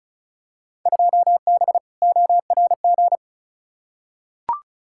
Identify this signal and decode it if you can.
Morse code, heard as a keyed tone, about 34 wpm, 700 Hz. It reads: 2B ORG
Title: Field Day Exchanges Encore - 35wpm